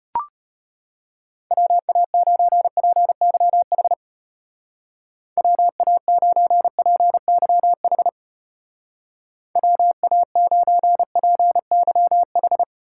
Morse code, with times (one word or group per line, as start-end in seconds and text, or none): 1.5-3.9 WA9PYH
5.4-8.1 WA9PYH
9.6-12.6 WA9PYH